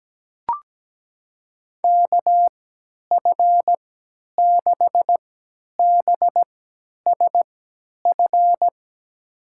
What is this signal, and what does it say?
Morse code, heard as a keyed tone, about 17 wpm, spaced as for 10 wpm, 700 Hz. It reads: KF6BSF
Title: USA Call Signs - 6 Characters - 3x Character Spacing - 17wpm